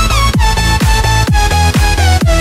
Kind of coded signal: none